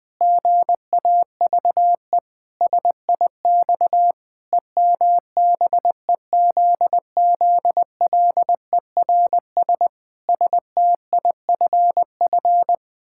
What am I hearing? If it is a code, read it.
Morse code, heard as a keyed tone, about 20 wpm, 700 Hz. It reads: GAVE SIX EMBEZZLERS STIFF
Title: Pangram Challenge v2 - 20wpm